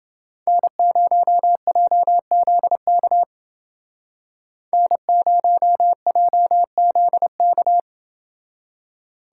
Morse code, heard as a keyed tone, about 29 wpm, 700 Hz. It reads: N0JZK N0JZK